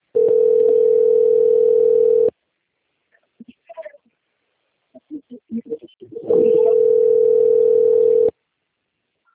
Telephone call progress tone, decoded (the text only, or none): ringback tone